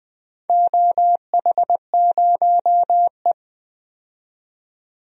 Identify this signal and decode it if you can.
Morse code, heard as a keyed tone, about 20 wpm, 700 Hz. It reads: OH0E
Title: DXCC-100 Call Signs - 4 Characters - 20wpm